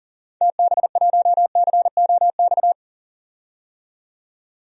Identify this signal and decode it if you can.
Morse code, heard as a keyed tone, about 40 wpm, 700 Hz. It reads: TB1COX
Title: DXCC-100 Call Signs - 6 Characters - 40wpm